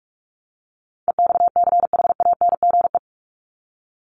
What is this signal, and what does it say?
Morse code, heard as a keyed tone, about 44 wpm, 700 Hz. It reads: EXCHANGE
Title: Top 751-1000 Words 45wpm